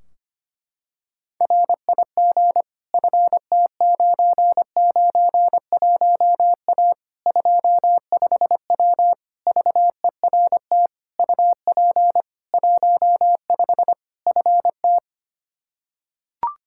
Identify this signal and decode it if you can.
Morse code, heard as a keyed tone, about 25 wpm, 700 Hz. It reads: RIG FT991A 25W VERT UP 15 FT
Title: Practice QSO - Exchange 2 - Info - 25wpm